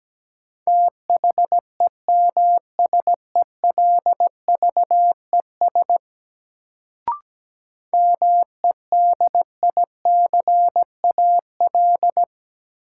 Morse code, heard as a keyed tone, about 17 wpm, 700 Hz. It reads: THEMSELVES MEDICAL
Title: Sentences from Top 800 Words - New Word Drill - 17wpm